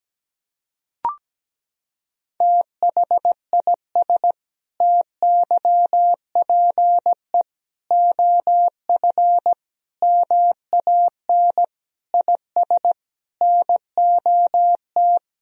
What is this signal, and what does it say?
Morse code, heard as a keyed tone, about 17 wpm, 700 Hz. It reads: THIS TYPE OF MAN IS NOT
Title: Sentences from Top 1000 Words - Review Part 3 - 17wpm